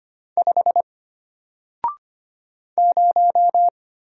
Morse code, heard as a keyed tone, about 25 wpm, 700 Hz. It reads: 5 0
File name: Single-Letter-Number-25wpm